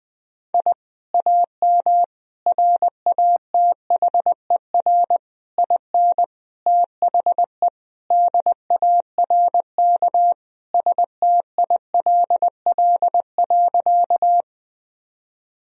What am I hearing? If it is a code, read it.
Morse code, heard as a keyed tone, about 20 wpm, 700 Hz. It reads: I AM RATHER IN THE DARK STILL.
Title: Sentences from Top 1000 Words - Review Part 4 - 20wpm